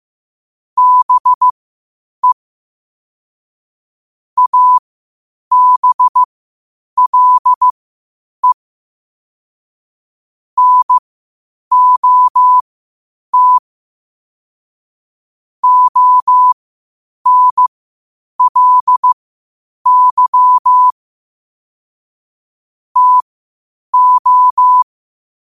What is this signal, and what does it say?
Morse code, heard as a keyed tone, about 15 wpm, spaced as for 8 wpm, 1000 Hz. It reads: BE ABLE NOT ONLY TO
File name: qotdfiles / 8wpm